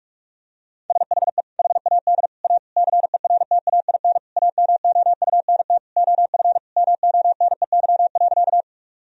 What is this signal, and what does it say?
Morse code, heard as a keyed tone, about 44 wpm, 700 Hz. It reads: SHE HAD A CERTAIN AMOUNT OF MONEY.
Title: Sentences from Top 1000 Words - Review Part 4 - 45wpm